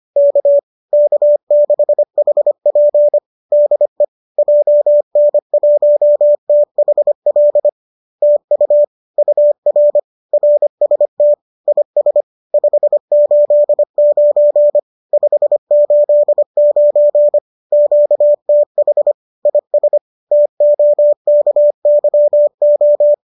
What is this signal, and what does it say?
Morse code, heard as a keyed tone, about 25 wpm, 590 Hz. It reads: K K6HP DE JN1THL TU UR RST IS 589 589 QTH IS TOKYO
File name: qso205 25